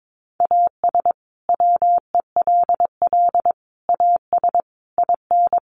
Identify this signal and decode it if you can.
Morse code, heard as a keyed tone, about 22 wpm, 700 Hz. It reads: AS WELL AS IN
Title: Sets of 5 Words Encore 22wpm